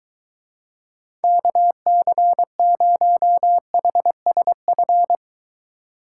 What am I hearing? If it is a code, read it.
Morse code, heard as a keyed tone, about 24 wpm, 700 Hz. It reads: KC0HSF